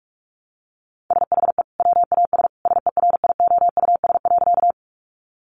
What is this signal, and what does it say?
Morse code, heard as a keyed tone, about 44 wpm, 700 Hz. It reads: SHE WAS SERIOUS.
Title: Sentences from Top 1100 Words 45wpm